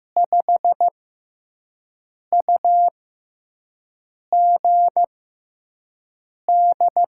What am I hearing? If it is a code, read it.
Morse code, heard as a keyed tone, about 15 wpm, spaced as for 5 wpm, 700 Hz. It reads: 5UGD
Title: USA Call Signs - 6 Characters - 6x Character Spacing - 15wpm